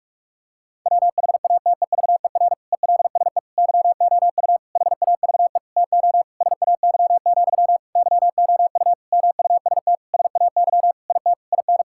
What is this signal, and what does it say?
Morse code, heard as a keyed tone, about 41 wpm, 700 Hz. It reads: WHATEVER ELSE YOU HAVE TO SAY, YOU MUST SAY IT IN
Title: Sentences from Top 1600 Words 45wpm